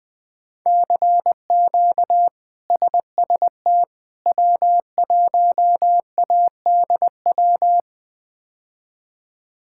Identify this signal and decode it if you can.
Morse code, heard as a keyed tone, about 20 wpm, 700 Hz. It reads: CQ SST W1ADW
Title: K1USN SST QSO Practice - 20wpm